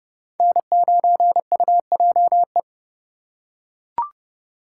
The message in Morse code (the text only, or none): N9UJE